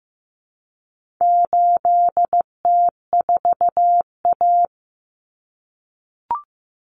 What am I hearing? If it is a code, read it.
Morse code, heard as a keyed tone, about 15 wpm, 700 Hz. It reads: 8T4A
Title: Common Cut Numbers - 4-Digit 15wpm